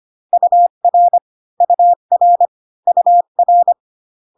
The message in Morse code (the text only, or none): UR UR UR